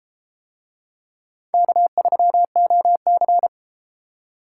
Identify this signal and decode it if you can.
Morse code, heard as a keyed tone, about 32 wpm, 700 Hz. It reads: K3OC